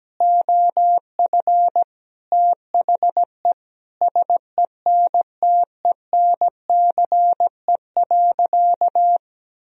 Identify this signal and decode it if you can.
Morse code, heard as a keyed tone, about 17 wpm, 700 Hz. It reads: OF THE SENTENCE.
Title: Sentences from Top 1000 Words - Review Part 1 - 17wpm